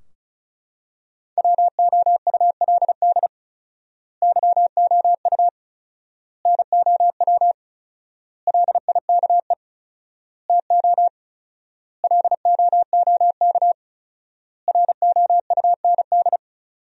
Morse code, heard as a keyed tone, about 35 wpm, 700 Hz. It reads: WOULD YOU NOW LIKE TO LOOK ROUND
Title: 4x Word-Spacing - Sentences from Top 200 Words - 35wpm